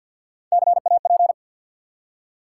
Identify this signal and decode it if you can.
Morse code, heard as a keyed tone, about 54 wpm, 700 Hz. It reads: CAP
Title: Three Letter Words - 50wpm